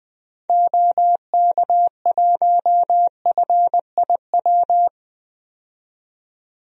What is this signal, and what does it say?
Morse code, heard as a keyed tone, about 20 wpm, 700 Hz. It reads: OK1FIW
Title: DXCC-100 Call Signs - 6 Characters - 20wpm